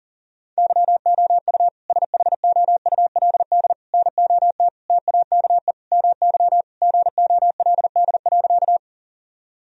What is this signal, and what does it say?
Morse code, heard as a keyed tone, about 40 wpm, 700 Hz. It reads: YOU SHOULD NOT TAKE MY GOLD.
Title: Sentences from Top 1000 Words - Review Part 3 - 40wpm